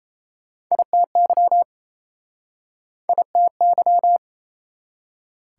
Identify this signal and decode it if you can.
Morse code, heard as a keyed tone, about 41 wpm, spaced as for 36 wpm, 700 Hz. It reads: ITY ITY